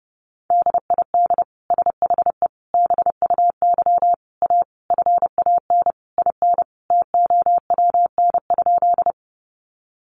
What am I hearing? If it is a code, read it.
Morse code, heard as a keyed tone, about 30 wpm, 700 Hz. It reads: DID SHE BUY A FAN IN TOWN?
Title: Sentences from Top 1300 Words 30wpm